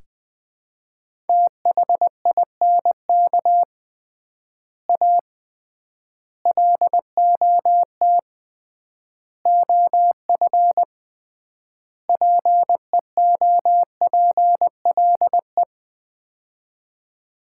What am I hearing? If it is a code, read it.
Morse code, heard as a keyed tone, about 21 wpm, 700 Hz. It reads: THINK A LOT OF PEOPLE
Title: 3x Word-Spacing - Sets of 5 Words - 20wpm